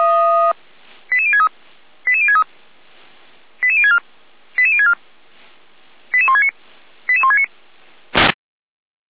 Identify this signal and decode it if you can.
Five-tone selective call (ZVEI), heard as quick runs of five tones, tones of about 70 ms. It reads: 88063 88063 88064 88064 88169 88169
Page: ZVEI